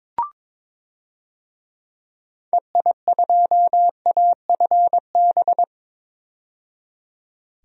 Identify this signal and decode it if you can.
Morse code, heard as a keyed tone, about 22 wpm, 700 Hz. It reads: EI2AFB
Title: DXCC-100 Call Signs - 6 Characters - 22wpm